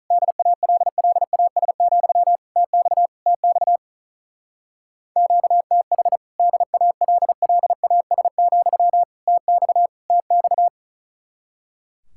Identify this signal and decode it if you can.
Morse code, heard as a keyed tone, about 44 wpm, 700 Hz. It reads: DALLAS, TX TX QTH DALLAS, TX TX